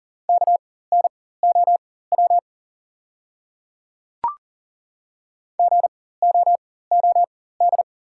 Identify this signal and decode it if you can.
Morse code, heard as a keyed tone, about 40 wpm, spaced as for 19 wpm, 690 Hz. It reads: KNOW GOOD
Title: 4x Character Spacing - Top 200 Words - 40wpm